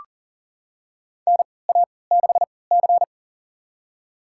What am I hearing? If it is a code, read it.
Morse code, heard as a keyed tone, about 40 wpm, spaced as for 23 wpm, 700 Hz. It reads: NA6C
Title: USA Call Signs - 4 Characters - 3x Character Spacing - 40wpm